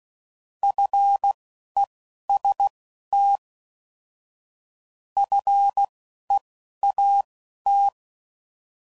Morse code, heard as a keyed tone, about 16 wpm, spaced as for 11 wpm, 800 Hz. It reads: FEST FEAT